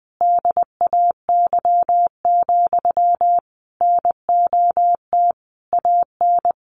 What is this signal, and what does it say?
Morse code, heard as a keyed tone, about 20 wpm, 700 Hz. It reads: DAY, NOT AN
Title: Sentences from Top 1400 Words 20wpm